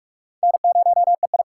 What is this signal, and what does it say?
Morse code, heard as a keyed tone, about 44 wpm, 700 Hz. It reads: N0EI
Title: USA Call Signs - 4 Characters - 4x Character Spacing - 45wpm